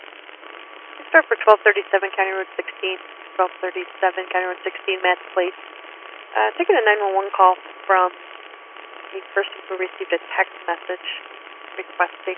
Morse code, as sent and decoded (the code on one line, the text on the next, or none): none